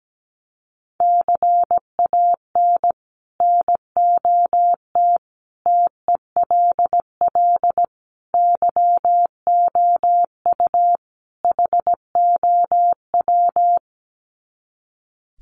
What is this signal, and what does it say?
Morse code, heard as a keyed tone, about 17 wpm, 700 Hz. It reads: CAN NOT TELL YOU HOW
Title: Sets of 5 Words Encore2 17wpm